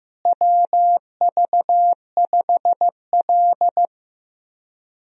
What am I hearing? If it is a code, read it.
Morse code, heard as a keyed tone, about 15 wpm, 700 Hz. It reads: WV5L
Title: USA Call Signs - 4 Characters - Repeated 2x - 15wpm